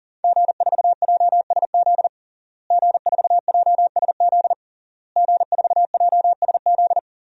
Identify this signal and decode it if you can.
Morse code, heard as a keyed tone, about 40 wpm, 700 Hz. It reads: G4JSZ G4JSZ G4JSZ
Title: DXCC-100 Call Signs - 5 Characters - Repeated 3x - 40wpm